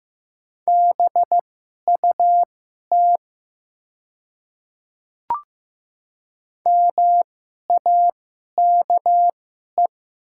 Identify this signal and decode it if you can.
Morse code, heard as a keyed tone, about 15 wpm, spaced as for 11 wpm, 700 Hz. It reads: BUT MAKE
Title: 2x Character Spacing - Top 200 Words - 15wpm